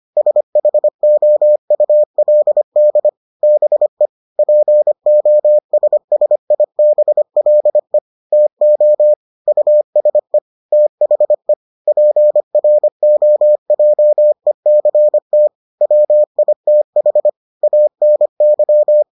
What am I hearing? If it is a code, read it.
Morse code, heard as a keyed tone, about 25 wpm, 600 Hz. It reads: SHOULD BE POSSIBLE TO USE THE PROJECT WITH ANY